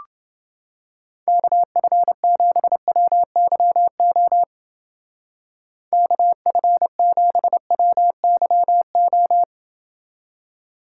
Morse code, heard as a keyed tone, about 28 wpm, 700 Hz. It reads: KF7WYO KF7WYO